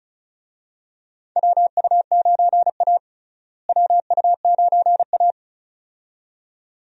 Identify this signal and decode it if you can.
Morse code, heard as a keyed tone, about 36 wpm, 700 Hz. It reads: WU9A WU9A
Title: USA Call Signs - 4 Characters - Repeated 2x - 35wpm